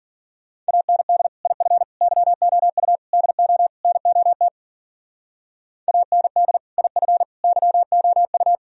Morse code, heard as a keyed tone, about 53 wpm, 700 Hz. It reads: AND IF YOU DO NOT AND IF YOU